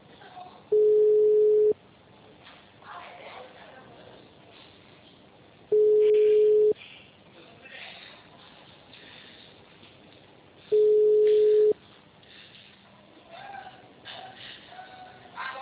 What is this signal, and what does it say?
Ringback tone (Europe, 425 Hz), with speech.